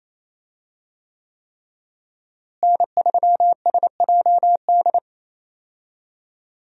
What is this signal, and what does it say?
Morse code, heard as a keyed tone, about 28 wpm, 700 Hz. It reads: N3SJD
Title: USA Call Signs - 5 Characters - Repeated 3x - 28wpm